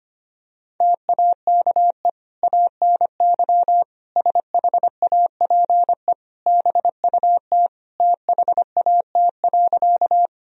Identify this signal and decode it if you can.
Morse code, heard as a keyed tone, about 25 wpm, 700 Hz. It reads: TAKE ANY SHAPE BUT THAT.
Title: Sentences from Top 1000 Words - Review Part 3 - 25wpm